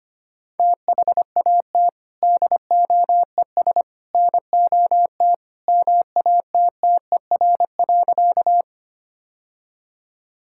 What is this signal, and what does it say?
Morse code, heard as a keyed tone, about 25 wpm, 700 Hz. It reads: THAT DOES NOT MATTER.
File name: practice / 25wpm / Sentences with 3-Word Phrases - 25wpm